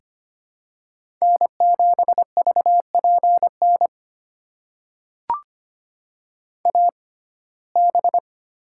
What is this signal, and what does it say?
Morse code, heard as a keyed tone, about 25 wpm, spaced as for 18 wpm, 700 Hz. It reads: N7VPN AB
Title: USA Call Signs - 5 Characters - 6x Character Spacing - 25wpm